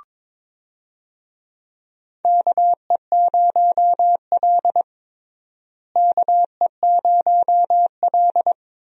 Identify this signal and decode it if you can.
Morse code, heard as a keyed tone, about 22 wpm, 700 Hz. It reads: KE0L KE0L
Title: USA Call Signs - 4 Characters - Repeated 3x - 22wpm